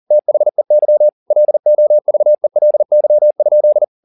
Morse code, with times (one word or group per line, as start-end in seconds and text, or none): 0.1-1.1 THEY
1.3-3.8 LOVELY?